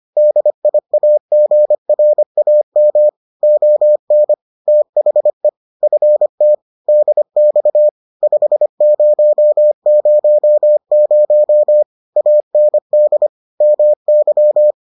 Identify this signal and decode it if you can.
Morse code, heard as a keyed tone, about 25 wpm, 600 Hz. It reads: DIAGRAM ON THE FT DX 5000 AND MY